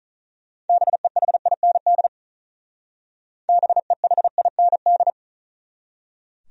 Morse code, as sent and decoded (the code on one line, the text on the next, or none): -... .... .. -. -.. / -... .... .. -. -..
BEHIND BEHIND